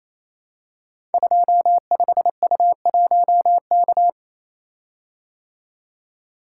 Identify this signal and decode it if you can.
Morse code, heard as a keyed tone, about 27 wpm, 700 Hz. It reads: 25U1K
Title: Code Groups - 5 Letter:Numbers - 28wpm